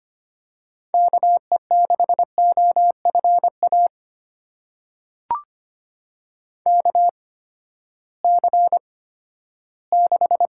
Morse code, heard as a keyed tone, about 25 wpm, spaced as for 14 wpm, 700 Hz. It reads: KE6OFA KC6
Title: USA Call Signs - 6 Characters - 8x Character Spacing - 25wpm